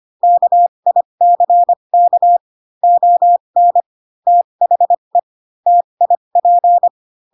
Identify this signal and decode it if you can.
Morse code, heard as a keyed tone, about 25 wpm, spaced as for 22 wpm, 700 Hz. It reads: KICK ON THE TIP
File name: Pinocchio / Pinocchio ch3 18 25 ik1hge